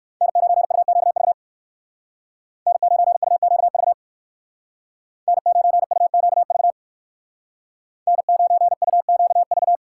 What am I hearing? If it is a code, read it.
Morse code, heard as a keyed tone, about 63 wpm, 700 Hz. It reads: N9UQV N9UQV N9UQV N9UQV